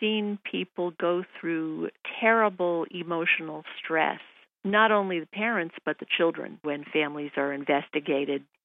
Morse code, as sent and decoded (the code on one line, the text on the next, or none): none